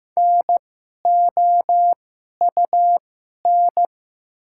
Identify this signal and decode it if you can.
Morse code, heard as a keyed tone, about 15 wpm, spaced as for 11 wpm, 700 Hz. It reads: NOUN
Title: 2x Character Spacing - Top 500 Words - 15wpm